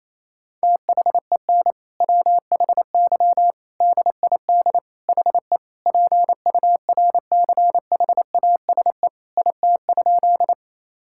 Morse code, heard as a keyed tone, about 28 wpm, 700 Hz. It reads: THEN WHY DID HE PURCHASE IT?
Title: Sentences from Top 1000 Words - Review Part 5 - 28wpm